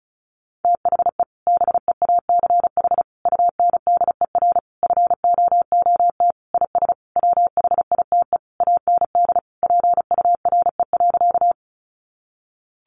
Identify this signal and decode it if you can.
Morse code, heard as a keyed tone, about 34 wpm, 700 Hz. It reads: THE BEACH UNDER FOOT IS WHITE AND PURE.
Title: Sentences from Top 1500 Words 35wpm